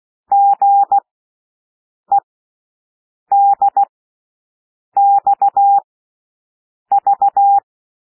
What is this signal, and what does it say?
Morse code, heard as a keyed tone, about 16 wpm, spaced as for 6 wpm, 800 Hz. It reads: GEDXV
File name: cwmp3 / L1 SOUND LETTERS 16 wpm